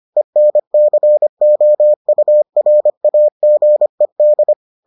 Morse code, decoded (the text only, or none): ENCOURAGED